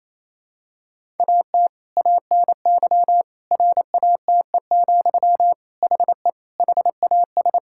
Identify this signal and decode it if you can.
Morse code, heard as a keyed tone, about 27 wpm, 700 Hz. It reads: AT ANY RATE, HE HAS
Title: Sentences from Top 1000 Words - Review Part 5 - 28wpm